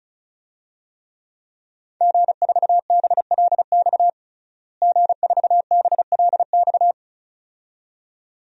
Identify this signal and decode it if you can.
Morse code, heard as a keyed tone, about 34 wpm, 700 Hz. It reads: G4BLX G4BLX
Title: DXCC-100 Call Signs - 5 Characters - Repeated 2x - 35wpm